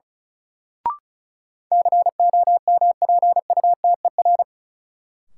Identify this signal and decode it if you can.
Morse code, heard as a keyed tone, about 36 wpm, 700 Hz. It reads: COMPUTER
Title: Sentences from Top 1000 Words - Review Part 3 - New Word Drill - 35wpm